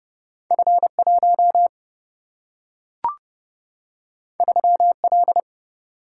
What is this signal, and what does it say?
Morse code, heard as a keyed tone, about 30 wpm, 700 Hz. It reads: F1 3L
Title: Single-Letter Single-Number 30wpm